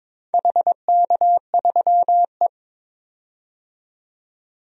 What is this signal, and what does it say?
Morse code, heard as a keyed tone, about 22 wpm, 700 Hz. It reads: HK3E
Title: DXCC-100 Call Signs - 4 Characters - 22wpm